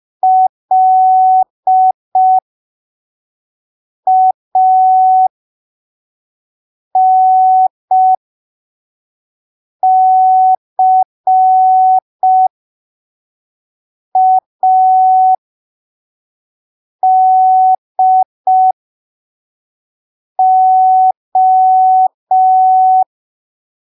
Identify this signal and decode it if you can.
Morse code, heard as a keyed tone, about 5 wpm, spaced as for 3 wpm, 750 Hz. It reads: LANCADO